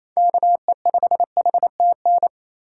Morse code, heard as a keyed tone, about 28 wpm, 700 Hz. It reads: KE5HTN
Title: USA Call Signs - 8x Character Spacing - 28wpm